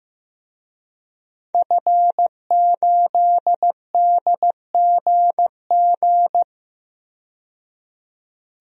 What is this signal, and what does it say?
Morse code, heard as a keyed tone, about 15 wpm, 700 Hz. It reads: F8DGG
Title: DXCC-100 Call Signs - 5 Characters - 15wpm